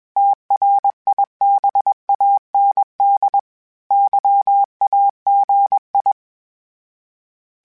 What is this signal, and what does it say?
Morse code, heard as a keyed tone, about 21 wpm, 800 Hz. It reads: TRIBAND YAGI
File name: fromSMs / FromSMs mp3s / Antennas@20